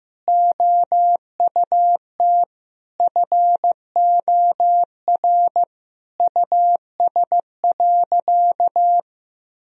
Morse code, heard as a keyed tone, about 15 wpm, 700 Hz. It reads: OUT FOR US.
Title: Sentences from Top 1000 Words - Review Part 2 - 15wpm